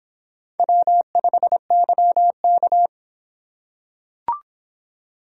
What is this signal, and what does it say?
Morse code, heard as a keyed tone, about 25 wpm, 700 Hz. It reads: W5YK